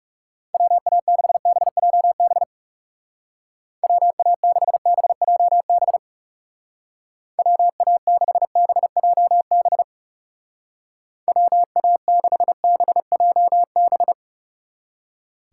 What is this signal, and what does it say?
Morse code, heard as a keyed tone, about 36 wpm, 700 Hz. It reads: WA6BJB WA6BJB WA6BJB WA6BJB
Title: Speed Racing - USA Call Signs 30wpm